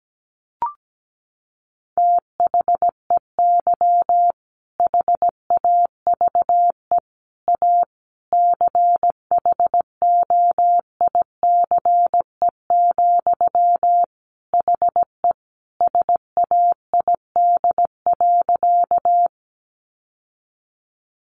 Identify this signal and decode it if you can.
Morse code, heard as a keyed tone, about 17 wpm, 700 Hz. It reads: THEY HAVE A CHOICE, HE SAID.
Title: Sentences from Top 900 Words 17wpm